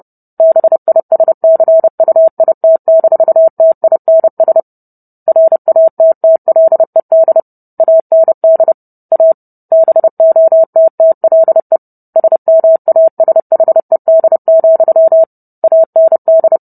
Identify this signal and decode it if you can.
Morse code, heard as a keyed tone, about 30 wpm, 650 Hz. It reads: BISCUIT-TINS RATTLED AND A BOTTLE SMASHED, AND